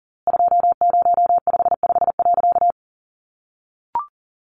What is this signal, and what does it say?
Morse code, heard as a keyed tone, about 40 wpm, 700 Hz. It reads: 2055.
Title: A Sound Of Thunder 40wpm